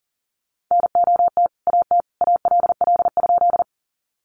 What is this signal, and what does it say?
Morse code, heard as a keyed tone, about 40 wpm, 700 Hz. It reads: NOT AT ALL?
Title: Sentences from Top 600 Words 40wpm